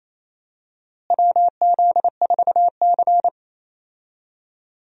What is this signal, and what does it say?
Morse code, heard as a keyed tone, about 28 wpm, 700 Hz. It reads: WZ4C
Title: USA Call Signs - 4 Characters - Repeated 2x - 28wpm